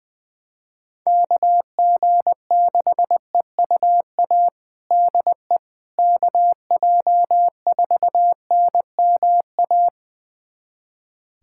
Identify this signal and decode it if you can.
Morse code, heard as a keyed tone, about 20 wpm, 700 Hz. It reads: KG6EUA DE KJ4NMA
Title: USA Callsign DE Callsign - 6 Characters - 20wpm